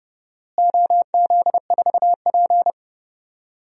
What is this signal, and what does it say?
Morse code, heard as a keyed tone, about 31 wpm, 700 Hz. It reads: OZ4P
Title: DXCC-100 Call Signs - 4 Characters - Repeated 2x - 30wpm